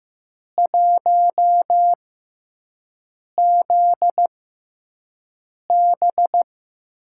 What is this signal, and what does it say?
Morse code, heard as a keyed tone, about 15 wpm, spaced as for 5 wpm, 700 Hz. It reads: 1ZB
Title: USA Call Signs - 6x Character Spacing - 15wpm